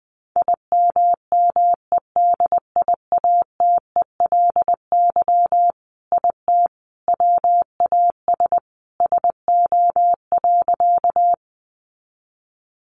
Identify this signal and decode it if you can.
Morse code, heard as a keyed tone, about 20 wpm, 700 Hz. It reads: IMMEDIATELY IT WAS SO.
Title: Sentences from Top 1100 Words 20wpm